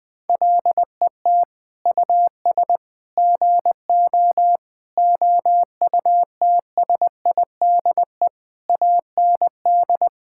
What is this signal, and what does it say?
Morse code, heard as a keyed tone, about 20 wpm, 700 Hz. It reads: LET US GO OUTSIDE AND
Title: Sentences from Top 1000 Words - Review Part 6 - 20wpm